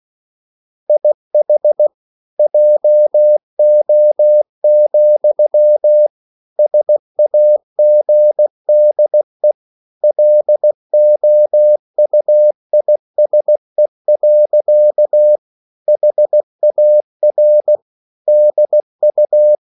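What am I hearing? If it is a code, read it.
Morse code, heard as a keyed tone, about 16 wpm, 600 Hz. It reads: IH JO, SAGDE LOUISE. HAR DU
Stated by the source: Slow (QRS) Morse code at 16 wpm, from CW med Gnister.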